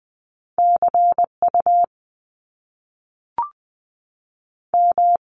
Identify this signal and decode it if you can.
Morse code, heard as a keyed tone, about 20 wpm, 700 Hz. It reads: C U M
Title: Top Two Letter Combinations in Words 20wpm